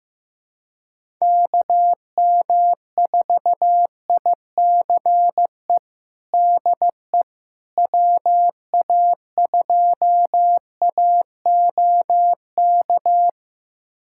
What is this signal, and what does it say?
Morse code, heard as a keyed tone, about 15 wpm, 700 Hz. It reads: KM4ICE DE WA2AOK
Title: USA Callsign DE Callsign - 6 Characters - 15wpm